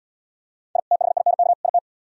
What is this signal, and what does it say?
Morse code, heard as a keyed tone, about 77 wpm, 700 Hz. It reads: I THINK IT